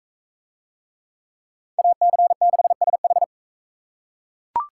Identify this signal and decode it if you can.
Morse code, heard as a keyed tone, about 40 wpm, 700 Hz. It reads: AC6SH